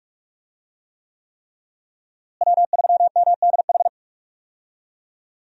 Morse code, heard as a keyed tone, about 46 wpm, 700 Hz. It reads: W3MDH
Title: USA Call Signs - 5 Characters - Repeated 3x - 45wpm